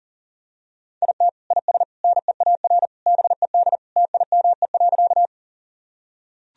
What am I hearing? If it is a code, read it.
Morse code, heard as a keyed tone, about 40 wpm, 700 Hz. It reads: IT IS NEAR BED TIME.
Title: Sentences from Top 1000 Words - Review Part 1 - 40wpm